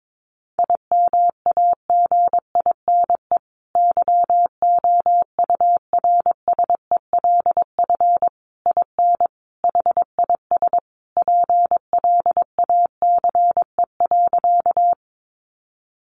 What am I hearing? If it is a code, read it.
Morse code, heard as a keyed tone, about 22 wpm, 700 Hz. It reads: IMAGINE YOURSELF IN HIS PLACE.
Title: Sentences from Top 1400 Words 22wpm